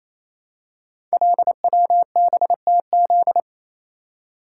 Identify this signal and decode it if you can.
Morse code, heard as a keyed tone, about 27 wpm, 700 Hz. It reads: LWBTZ
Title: Code Groups - 5 Letter:Numbers - 28wpm